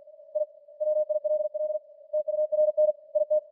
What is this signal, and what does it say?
Morse code, heard as a keyed tone, about 45 wpm, 610 Hz. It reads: I WILL TURN IT